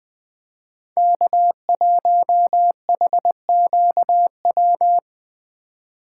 Morse code, heard as a keyed tone, about 20 wpm, 700 Hz. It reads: K1HQW
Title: USA Call Signs - 5 Characters - 4x Character Spacing - 20wpm